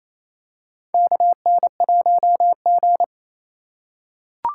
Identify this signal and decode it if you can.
Morse code, heard as a keyed tone, about 28 wpm, 700 Hz. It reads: KN1G
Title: USA Call Signs - 3x Character Spacing - 28wpm